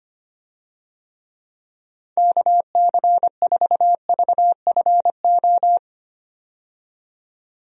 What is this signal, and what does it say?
Morse code, heard as a keyed tone, about 25 wpm, 700 Hz. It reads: KC4VFO
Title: USA Call Signs - 6 Characters - 25wpm